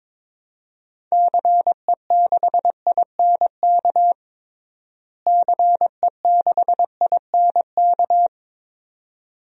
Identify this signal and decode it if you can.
Morse code, heard as a keyed tone, about 22 wpm, 700 Hz. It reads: CE6INK CE6INK
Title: DXCC-100 Call Signs - 6 Characters - Repeated 2x - 22wpm